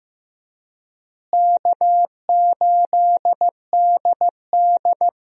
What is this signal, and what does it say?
Morse code, heard as a keyed tone, about 15 wpm, 700 Hz. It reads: K8DD
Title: USA Call Signs - 4 Characters - Repeated 3x - 15wpm